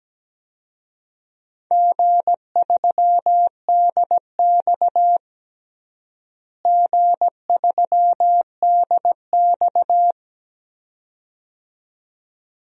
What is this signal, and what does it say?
Morse code, heard as a keyed tone, about 17 wpm, 700 Hz. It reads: G3DX G3DX
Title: DXCC-100 Call Signs - 4 Characters - Repeated 2x - 17wpm